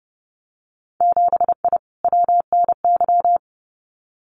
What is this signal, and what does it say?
Morse code, heard as a keyed tone, about 30 wpm, 700 Hz. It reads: 7I WNY
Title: Winter Field Day Exchanges Encore 30wpm